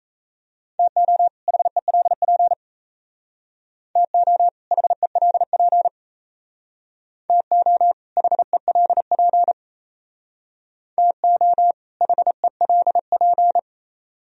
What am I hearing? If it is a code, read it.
Morse code, heard as a keyed tone, about 36 wpm, 700 Hz. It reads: TO HELP TO HELP TO HELP TO HELP